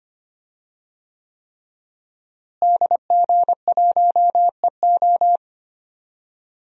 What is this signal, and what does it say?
Morse code, heard as a keyed tone, about 25 wpm, 700 Hz. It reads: DG1EO
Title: DXCC-100 Call Signs - 5 Characters - Repeated 3x - 25wpm